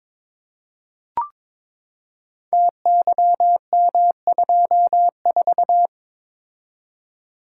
Morse code, heard as a keyed tone, about 22 wpm, 700 Hz. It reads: TYM24
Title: Code Groups - 5 Letter:Numbers - 22wpm